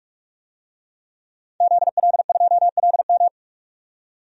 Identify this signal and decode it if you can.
Morse code, heard as a keyed tone, about 47 wpm, 700 Hz. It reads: ZL2LM
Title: DXCC-100 Call Signs - 5 Characters - 45wpm